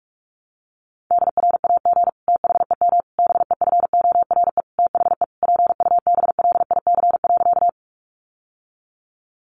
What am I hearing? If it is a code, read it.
Morse code, heard as a keyed tone, about 44 wpm, 700 Hz. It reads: DRAG THEM BEFORE THE PUBLIC.
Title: Sentences from Top 1500 Words 45wpm